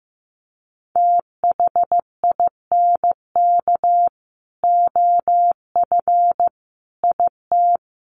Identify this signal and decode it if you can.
Morse code, heard as a keyed tone, about 15 wpm, 700 Hz. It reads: THINK OF IT
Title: Sets of 5 Words Encore2 15wpm